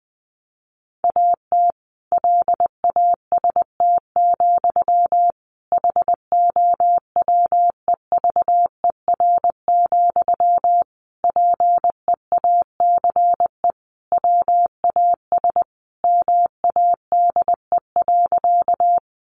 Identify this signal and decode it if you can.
Morse code, heard as a keyed tone, about 20 wpm, 700 Hz. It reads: AT LAST, HOWEVER, PEACE WAS MADE.
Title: Sentences from Top 1500 Words 20wpm